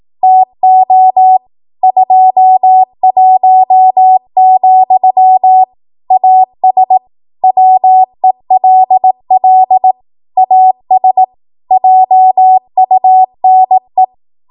Morse code, heard as a keyed tone, about 18 wpm, 750 Hz. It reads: TO 21, AS WELL AS JUNE
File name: Morse / 18WPM